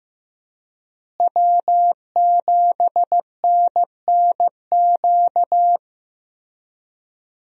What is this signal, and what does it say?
Morse code, heard as a keyed tone, about 15 wpm, 700 Hz. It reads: W7NNQ
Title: USA Call Signs - 5 Characters - Repeated 2x - 15wpm